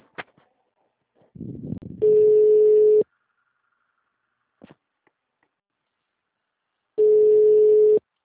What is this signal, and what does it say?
Ringback tone (Europe, 425 Hz).